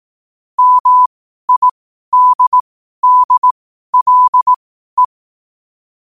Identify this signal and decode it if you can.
Morse code, heard as a keyed tone, about 18 wpm, spaced as for 12 wpm, 1000 Hz. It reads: MIDDLE
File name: qotdfiles / 12wpm